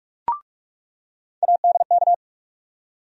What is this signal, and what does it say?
Morse code, heard as a keyed tone, about 47 wpm, 700 Hz. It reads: ADK
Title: USA Call Sign Postfixes - 3 Letters 45wpm